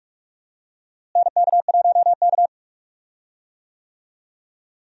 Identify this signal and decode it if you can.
Morse code, heard as a keyed tone, about 44 wpm, 700 Hz. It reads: NK1K
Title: USA Call Signs - 4 Characters - Repeated 2x - 45wpm